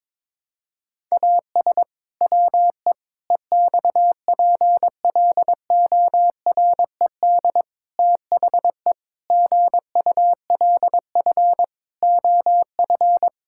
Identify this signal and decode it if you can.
Morse code, heard as a keyed tone, about 21 wpm, 700 Hz. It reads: AS WE EXPLORED THE GULF OF
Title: Pangram Challenge v2 - 22wpm